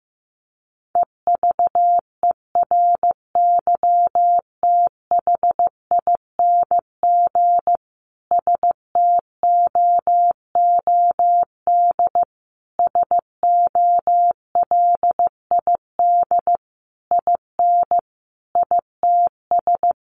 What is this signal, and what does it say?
Morse code, heard as a keyed tone, about 15 wpm, 700 Hz. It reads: EVERYTHING STOOD SOLID IN ITS
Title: Sentences from Top 1500 Words 15wpm